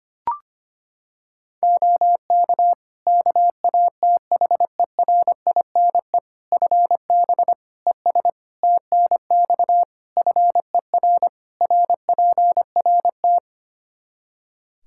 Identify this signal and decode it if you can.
Morse code, heard as a keyed tone, about 25 wpm, 700 Hz. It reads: OK KATHERINE FB ES TNX FER RPRT
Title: K1USN SST QSO Practice 25wpm